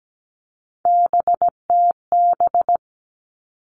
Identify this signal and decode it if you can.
Morse code, heard as a keyed tone, about 17 wpm, 700 Hz. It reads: BTB
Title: All Cut Numbers - 3-Digit 17wpm